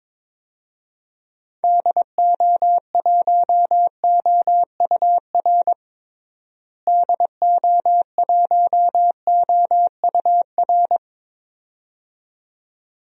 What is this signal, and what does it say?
Morse code, heard as a keyed tone, about 21 wpm, 700 Hz. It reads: DO1OUR DO1OUR
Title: DXCC-100 Call Signs - 6 Characters - Repeated 2x - 22wpm